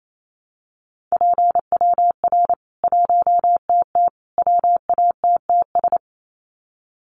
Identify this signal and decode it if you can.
Morse code, heard as a keyed tone, about 28 wpm, 700 Hz. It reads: PWR 1TT WATTS
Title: Common QSO Phrases 28wpm